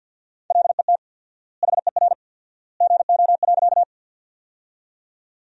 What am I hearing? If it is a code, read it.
Morse code, heard as a keyed tone, about 52 wpm, 700 Hz. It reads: LET HER GO.
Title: 4x Word-Spacing - Sentences from Top 300 Words - 50wpm